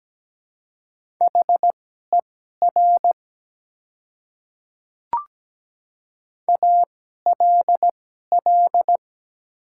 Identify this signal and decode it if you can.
Morse code, heard as a keyed tone, about 17 wpm, spaced as for 12 wpm, 700 Hz. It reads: HER ALL
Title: 2x Character Spacing - Top 100 Words - 17wpm